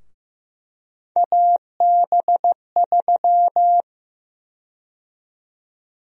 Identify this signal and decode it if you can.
Morse code, heard as a keyed tone, about 15 wpm, 700 Hz. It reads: AB3
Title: USA Call Sign Prefixes - 15wpm